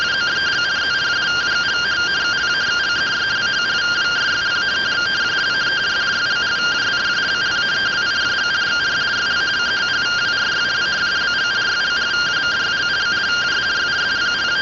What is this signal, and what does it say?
Two-tone teleprinter signal, about 170 Hz shift.